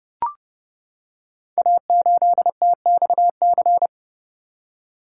Morse code, heard as a keyed tone, about 30 wpm, 700 Hz. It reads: A8TXC
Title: Code Groups - 5 Letter:Numbers - 30wpm